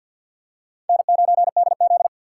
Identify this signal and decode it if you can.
Morse code, heard as a keyed tone, about 53 wpm, 700 Hz. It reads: N9DZ
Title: USA Call Signs - 4 Characters - 3x Character Spacing - 50wpm